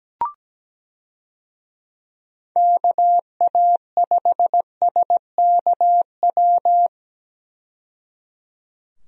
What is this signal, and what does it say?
Morse code, heard as a keyed tone, about 17 wpm, 700 Hz. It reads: KA5SKW
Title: USA Call Signs - 6 Characters - 17wpm